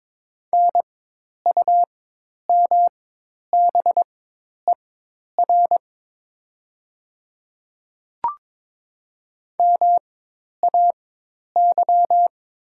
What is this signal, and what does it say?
Morse code, heard as a keyed tone, about 22 wpm, spaced as for 10 wpm, 700 Hz. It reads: NUMBER MAY
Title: 4x Character Spacing - Top 100 Words - 22wpm